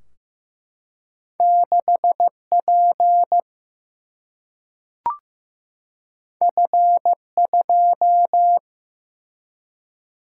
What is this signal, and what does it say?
Morse code, heard as a keyed tone, about 15 wpm, 700 Hz. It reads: 6 P F 2
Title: Single-Letter Single-Number 15wpm